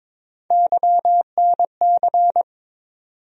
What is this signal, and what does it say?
Morse code, heard as a keyed tone, about 22 wpm, 700 Hz. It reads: YNC
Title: USA Call Sign Postfixes - 3 Letters 22wpm